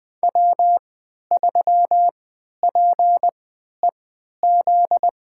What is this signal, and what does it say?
Morse code, heard as a keyed tone, about 20 wpm, spaced as for 11 wpm, 700 Hz. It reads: W3PEZ
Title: USA Call Signs - 5 Characters - 3x Character Spacing - 20wpm